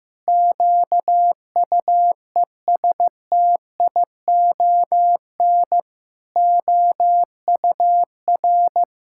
Morse code, heard as a keyed tone, about 15 wpm, 700 Hz. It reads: QUESTION OUR
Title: Sentences from Top 1000 Words - Review Part 5 - 15wpm